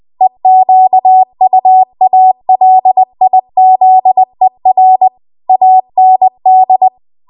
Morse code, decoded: EQUALIZER AND